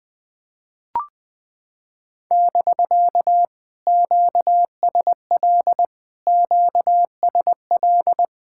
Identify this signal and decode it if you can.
Morse code, heard as a keyed tone, about 20 wpm, 700 Hz. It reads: <BK> QSL QSL
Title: POTA QSO Practice - 20wpm